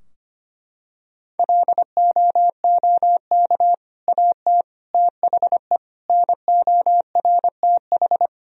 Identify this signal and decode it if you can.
Morse code, heard as a keyed tone, about 25 wpm, 700 Hz. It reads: LOOK AT THE NORTH
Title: Sentences from Top 1000 Words - Review Part 2 - 25wpm